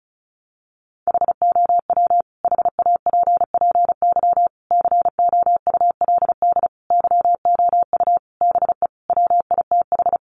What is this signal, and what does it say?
Morse code, heard as a keyed tone, about 34 wpm, 700 Hz. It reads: HOW HAPPY COULD YOU BE WITH
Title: Sentences from Top 1000 Words 35wpm